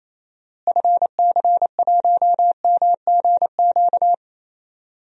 Morse code, heard as a keyed tone, about 28 wpm, 700 Hz. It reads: FC1MGQ
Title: DXCC-100 Call Signs - 6 Characters - Repeated 2x - 28wpm